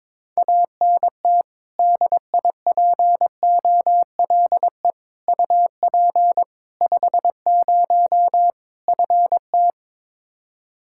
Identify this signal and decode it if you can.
Morse code, heard as a keyed tone, about 22 wpm, 700 Hz. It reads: ANT DIPOLE UP 50 FT
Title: Practice QSO - Exchange 2 - Info - 22wpm